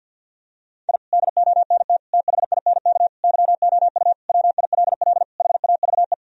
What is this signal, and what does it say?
Morse code, heard as a keyed tone, about 50 wpm, 700 Hz. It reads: I DONT THINK YOU WILL HAVE
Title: Sentences from Top 1000 Words - Review Part 4 - 50wpm